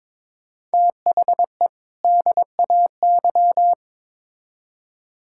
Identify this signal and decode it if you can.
Morse code, heard as a keyed tone, about 22 wpm, 700 Hz. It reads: THE DAY
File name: Build A Phrase - 2-Word Phrases - 22wpm